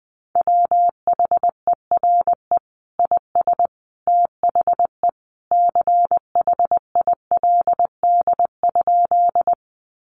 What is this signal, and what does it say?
Morse code, heard as a keyed tone, about 20 wpm, 700 Hz. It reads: WHERE IS THE CHILD?
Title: Sentences from Top 800 Words 20wpm